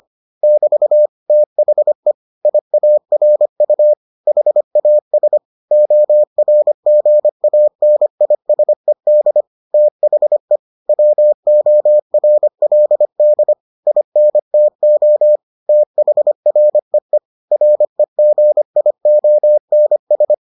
Morse code, heard as a keyed tone, about 25 wpm, 600 Hz. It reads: <BT> THE IARU HAS ORGANISED THE WORLD INTO THREE REGIONS